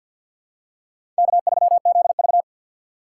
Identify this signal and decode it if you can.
Morse code, heard as a keyed tone, about 53 wpm, 700 Hz. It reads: K3ZV
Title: USA Call Signs - 6x Character Spacing - 50wpm